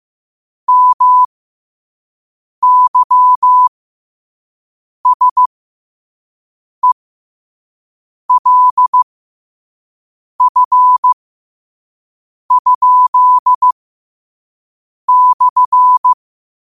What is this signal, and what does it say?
Morse code, heard as a keyed tone, about 15 wpm, spaced as for 5 wpm, 1000 Hz. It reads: MYSELF?/
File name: qotdfiles / 5wpm